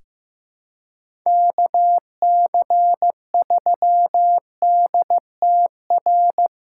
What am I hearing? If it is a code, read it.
Morse code, heard as a keyed tone, about 15 wpm, 700 Hz. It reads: KC3DTR
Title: USA Callsign DE Callsign 15wpm